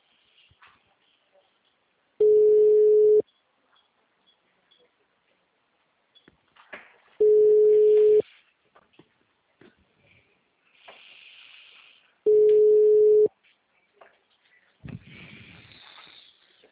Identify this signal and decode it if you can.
Ringback tone (Europe, 425 Hz).